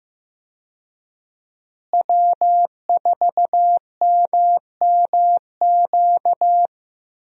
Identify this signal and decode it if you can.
Morse code, heard as a keyed tone, about 15 wpm, 700 Hz. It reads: W4MMQ
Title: USA Call Signs - 5 Characters - Repeated 3x - 15wpm